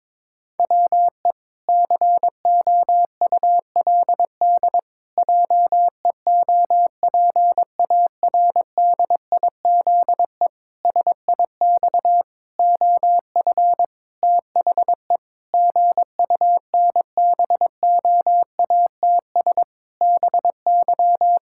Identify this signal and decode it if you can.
Morse code, heard as a keyed tone, about 22 wpm, 700 Hz. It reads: WE COULD JEOPARDIZE SIX OF THE GUNBOATS BY
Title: Pangram Challenge v2 - 22wpm